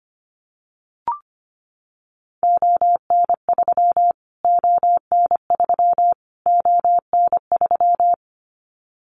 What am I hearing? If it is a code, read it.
Morse code, heard as a keyed tone, about 25 wpm, 700 Hz. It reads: ON3 ON3 ON3
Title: Callsign Prefixes for Top 25 DXCC - Repeated 3x - 25wpm